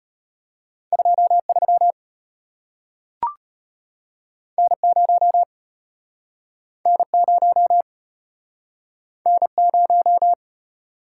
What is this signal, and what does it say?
Morse code, heard as a keyed tone, about 35 wpm, 700 Hz. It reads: 23 N0 N0 N0